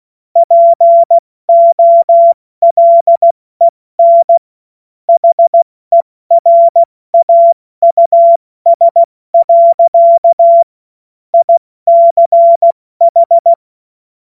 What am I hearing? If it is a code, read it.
Morse code, heard as a keyed tone, about 16 wpm, 680 Hz. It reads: POLEN HERAUS. ICH
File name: Morsekurs / Witze / Witz 28.November 2023 70 F80